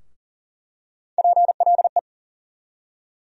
Morse code, heard as a keyed tone, about 40 wpm, 700 Hz. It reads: PLE
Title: USA Call Sign Postfixes - 3 Letters 40wpm